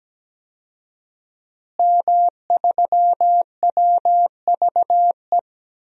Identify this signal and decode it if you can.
Morse code, heard as a keyed tone, about 17 wpm, 700 Hz. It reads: M3WVE
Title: DXCC-100 Call Signs - 5 Characters - Repeated 3x - 17wpm